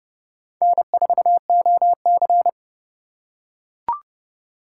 Morse code, heard as a keyed tone, about 29 wpm, 700 Hz. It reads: N4OC